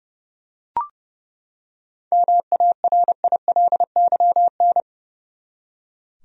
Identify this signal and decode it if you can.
Morse code, heard as a keyed tone, about 30 wpm, 700 Hz. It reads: MARILYN
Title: USA Common Names - 30wpm